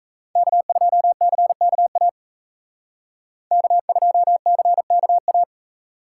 Morse code, heard as a keyed tone, about 37 wpm, 700 Hz. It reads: K2CKA K2CKA